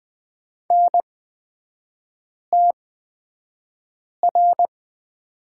Morse code, heard as a keyed tone, about 20 wpm, spaced as for 5 wpm, 700 Hz. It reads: NTR
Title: 224 Lesson 36e - BT - Three Characters 20wpm